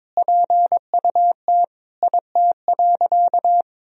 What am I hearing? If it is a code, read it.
Morse code, heard as a keyed tone, about 22 wpm, 700 Hz. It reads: PUT IT.
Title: Sentences from Top 200 Words and Abbreviations 22wpm